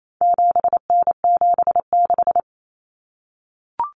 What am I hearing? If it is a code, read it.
Morse code, heard as a keyed tone, about 28 wpm, 700 Hz. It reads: 7N76
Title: Common Cut Numbers - 4-Digit 28wpm